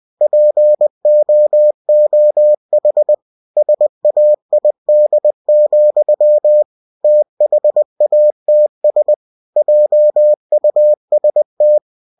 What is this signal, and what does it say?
Morse code, heard as a keyed tone, about 20 wpm, 600 Hz. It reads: POOH SAID, THATS JUST
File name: cw-books / winnie-the-pooh / winnie-20wpm-page-0034